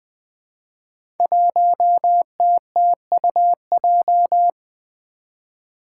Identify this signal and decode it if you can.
Morse code, heard as a keyed tone, about 20 wpm, 700 Hz. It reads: 1TTUJ
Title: Five Challenging Letter-Number Combinations 20wpm